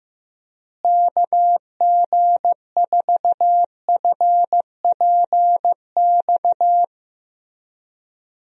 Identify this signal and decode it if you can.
Morse code, heard as a keyed tone, about 15 wpm, 700 Hz. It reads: KG4FPX
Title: USA Call Signs - 6 Characters - Repeated 3x - 15wpm